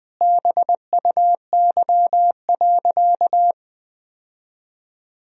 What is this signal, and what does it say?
Morse code, heard as a keyed tone, about 21 wpm, 700 Hz. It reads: BUY.
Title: Sentences from Top 1000 Words - Review Part 4 - 20wpm